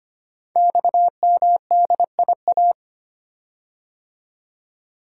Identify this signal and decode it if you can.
Morse code, heard as a keyed tone, about 25 wpm, 700 Hz. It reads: XMDIA
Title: Code Groups - 5 Letters - 25wpm